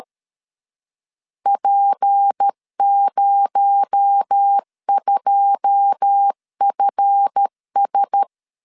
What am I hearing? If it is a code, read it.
Morse code, heard as a keyed tone, about 13 wpm, 800 Hz. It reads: P02FS
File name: LEZIONI / 9 2 12wpm 2301